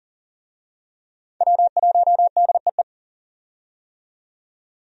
Morse code, heard as a keyed tone, about 40 wpm, 700 Hz. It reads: W1DEE
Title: USA Call Signs - 5 Characters - 40wpm